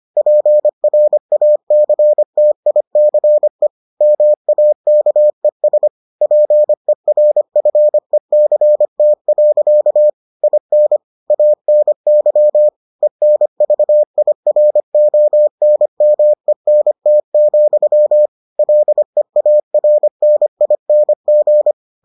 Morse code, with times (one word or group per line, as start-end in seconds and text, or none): 0.2-3.7 PRACTICE
4.0-5.9 MAKES
6.2-10.1 PERFECT.
10.4-11.0 IN
11.3-12.7 ANY
13.0-18.3 ENVIRONMENT,
18.6-21.7 LEARNING